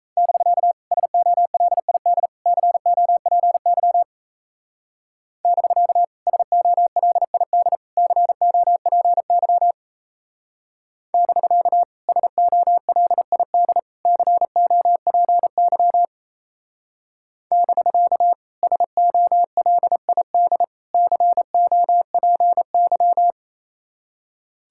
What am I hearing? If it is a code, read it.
Morse code, heard as a keyed tone, about 34 wpm, 700 Hz. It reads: <BK> SOLID COPY <BK> SOLID COPY <BK> SOLID COPY <BK> SOLID COPY
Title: Speed-Racing Common QSO Phrases Encore - 28wpm